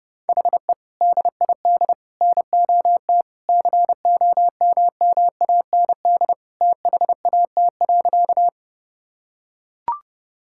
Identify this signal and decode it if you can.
Morse code, heard as a keyed tone, about 30 wpm, 700 Hz. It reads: HE DID NOT COMMAND THAT.
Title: Sentences from Top 1600 Words 30wpm